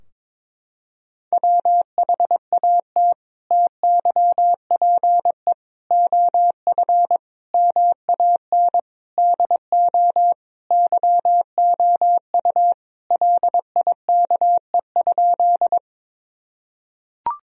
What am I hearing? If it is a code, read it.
Morse code, heard as a keyed tone, about 22 wpm, 700 Hz. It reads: WHAT TYPE OF MAN DO YOU LIKE?
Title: Sentences from Top 1000 Words - Review Part 3 - 22wpm